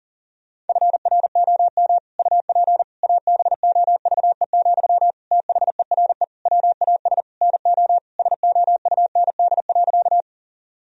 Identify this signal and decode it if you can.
Morse code, heard as a keyed tone, about 40 wpm, 700 Hz. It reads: FROM UP ABOVE, THERE WAS NO SOUND.
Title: Sentences from Top 1000 Words - Review Part 2 - 40wpm